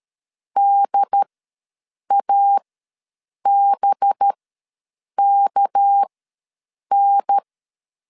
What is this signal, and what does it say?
Morse code, heard as a keyed tone, about 13 wpm, spaced as for 7 wpm, 800 Hz. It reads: DABKN
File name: LEZIONI / 7 GF9Z / 7 GF9 ws 12wpm 3344